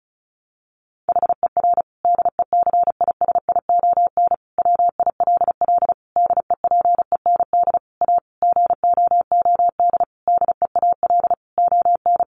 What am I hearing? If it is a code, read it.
Morse code, heard as a keyed tone, about 34 wpm, 700 Hz. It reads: HER DECISION WILL DEPEND A GOOD DEAL ON
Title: Sentences from Top 1300 Words 35wpm